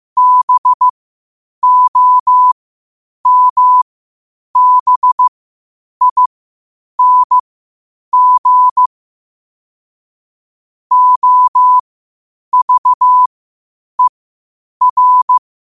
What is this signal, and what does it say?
Morse code, heard as a keyed tone, about 15 wpm, spaced as for 8 wpm, 1000 Hz. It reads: BOMBING OVER
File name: qotdfiles / 8wpm